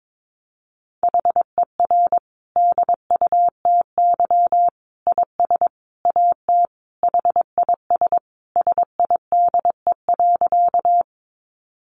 Morse code, heard as a keyed tone, about 22 wpm, 700 Hz. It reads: HER DUTY IS AT HIS SIDE.
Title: Sentences from Top 1500 Words 22wpm